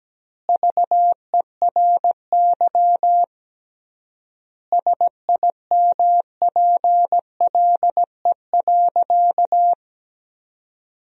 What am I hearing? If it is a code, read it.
Morse code, heard as a keyed tone, about 17 wpm, 700 Hz. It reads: VERY SIMPLE.
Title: 3x Word-Spacing - Sentences from Top 500 Words - 17wpm